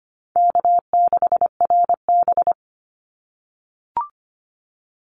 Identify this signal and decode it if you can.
Morse code, heard as a keyed tone, about 25 wpm, 700 Hz. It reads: K6RB
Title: Easy Call Signs - 25wpm